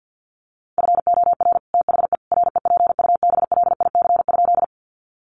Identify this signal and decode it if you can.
Morse code, heard as a keyed tone, about 50 wpm, 700 Hz. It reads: FOR THE REPUBLIC?
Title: Sentences from Top 1500 Words 50wpm